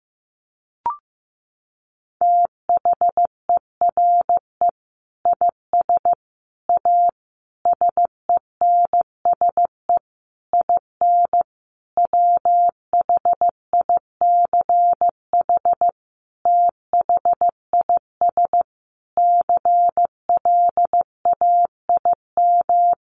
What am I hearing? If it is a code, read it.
Morse code, heard as a keyed tone, about 14 wpm, 700 Hz. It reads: THERE IS A SENSE IN WHICH THIS CLAIM
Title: Sentences from Top 1500 Words 15wpm